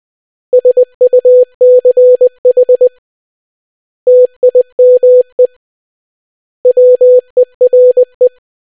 Morse code, heard as a keyed tone, about 20 wpm, 500 Hz. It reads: SUCH TIME WERE